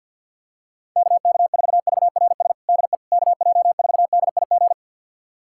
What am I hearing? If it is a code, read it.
Morse code, heard as a keyed tone, about 49 wpm, 700 Hz. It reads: KK4VRS DE KJ4DIG
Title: USA Callsign DE Callsign - 6 Characters - 50wpm